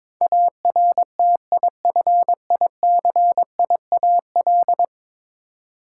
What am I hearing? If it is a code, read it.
Morse code, heard as a keyed tone, about 22 wpm, 700 Hz. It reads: ARTIFICIAL
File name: practice / 22wpm / 10 Letter Words - First 3 Letters Given - 22wpm